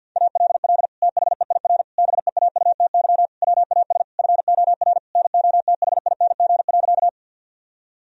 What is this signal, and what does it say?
Morse code, heard as a keyed tone, about 50 wpm, 700 Hz. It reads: ALL THEIR BEAUTY WAS FOR NOTHING.
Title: Sentences from Top 1000 Words - Review Part 2 - 50wpm